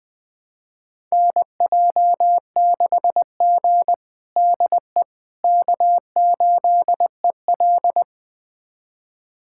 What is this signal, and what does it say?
Morse code, heard as a keyed tone, about 21 wpm, 700 Hz. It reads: NJ6G DE K8EL
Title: USA Callsign DE Callsign - 4 Characters - 20wpm